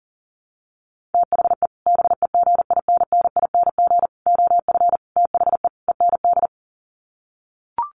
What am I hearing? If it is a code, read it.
Morse code, heard as a keyed tone, about 40 wpm, 700 Hz. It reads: THE BEGINNING OF THE END
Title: Sets of 5 Words Encore2 40wpm